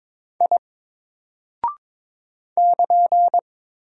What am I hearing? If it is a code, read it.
Morse code, heard as a keyed tone, about 22 wpm, 700 Hz. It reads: I <KN>
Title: Single Character (Letter:Number:Punctuation:Prosign) 22wpm